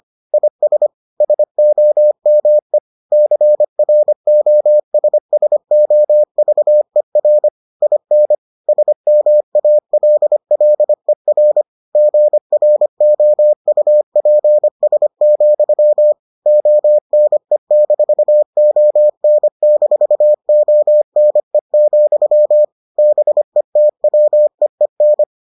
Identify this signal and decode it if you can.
Morse code, heard as a keyed tone, about 25 wpm, 600 Hz. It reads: IS SOME CROSSOVER IN SMALLER GROUPS, ONE-ON-ONE, BETWEEN